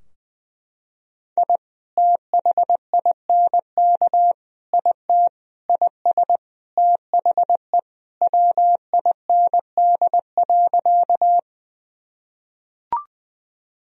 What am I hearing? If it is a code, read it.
Morse code, heard as a keyed tone, about 21 wpm, 700 Hz. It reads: I THINK IT IS THE WIND.
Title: Sentences from Top 1000 Words - Review Part 2 - 20wpm